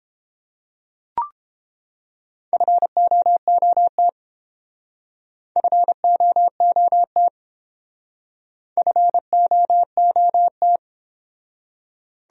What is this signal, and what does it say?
Morse code, heard as a keyed tone, about 27 wpm, 700 Hz. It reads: FOOT FOOT FOOT
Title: Speed-Racing Top 500 Words 22wpm